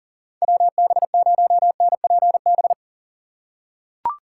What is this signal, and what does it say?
Morse code, heard as a keyed tone, about 39 wpm, 700 Hz. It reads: WB0NPB
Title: USA Call Signs - 6 Characters - 2x Character Spacing - 40wpm